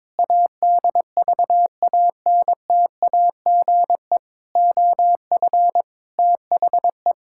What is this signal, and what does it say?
Morse code, heard as a keyed tone, about 22 wpm, 700 Hz. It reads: ADVANTAGE OF THE
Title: Sentences from Top 1000 Words - Review Part 6 - 22wpm